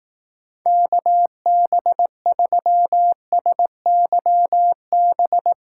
Morse code, heard as a keyed tone, about 18 wpm, 700 Hz. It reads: KB3SYB
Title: Speed Racing - USA Call Signs 15wpm